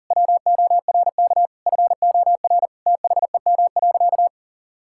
Morse code, heard as a keyed tone, about 40 wpm, 700 Hz. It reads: WORK FOR THEM.
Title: Sentences from Top 1000 Words - Review Part 5 - 40wpm